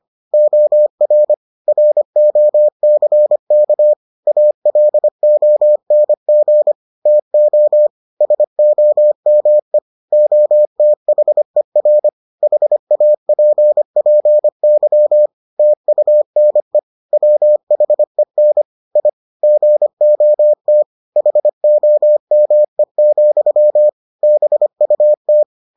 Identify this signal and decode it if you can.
Morse code, heard as a keyed tone, about 25 wpm, 600 Hz. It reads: OR ROCK ALONG TO SOME OTHER HAPPY TUNE WHEN I GOT HOME, BUT